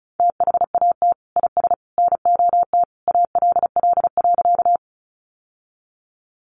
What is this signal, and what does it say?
Morse code, heard as a keyed tone, about 34 wpm, 700 Hz. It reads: THAT IS NOT ALL.
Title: Sentences from Top 600 Words 35wpm